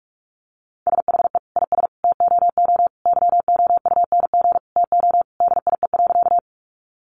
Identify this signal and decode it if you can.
Morse code, heard as a keyed tone, about 44 wpm, 700 Hz. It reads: SHE IS TOO YOUNG TO DIE.
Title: Sentences from Top 1300 Words 45wpm